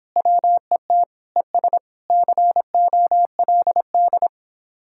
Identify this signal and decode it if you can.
Morse code, heard as a keyed tone, about 26 wpm, 700 Hz. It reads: WET ES COLD